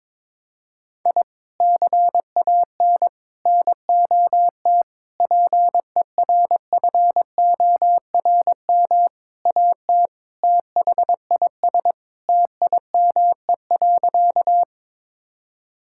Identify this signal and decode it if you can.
Morse code, heard as a keyed tone, about 22 wpm, 700 Hz. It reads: I CAN NOT PERFORM AT THIS TIME.
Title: Sentences from Top 1000 Words - Review Part 5 - 22wpm